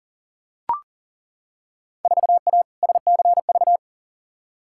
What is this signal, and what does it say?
Morse code, heard as a keyed tone, about 40 wpm, 700 Hz. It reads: 4A SCV
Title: Field Day Exchanges Encore - 40wpm